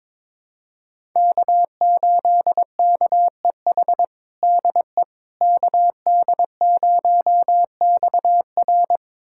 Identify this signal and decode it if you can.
Morse code, heard as a keyed tone, about 22 wpm, 700 Hz. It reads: K8KEH DE KD0XR
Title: USA Callsign DE Callsign - 5 Characters - 22wpm